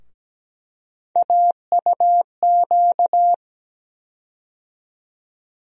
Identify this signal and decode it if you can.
Morse code, heard as a keyed tone, about 17 wpm, 700 Hz. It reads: AUQ
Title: USA Call Sign Postfixes - 3 Letters 17wpm